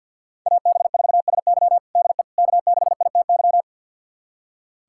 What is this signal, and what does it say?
Morse code, heard as a keyed tone, about 47 wpm, 700 Hz. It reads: AB4SY DE K6ITY
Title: USA Callsign DE Callsign - 5 Characters - 50wpm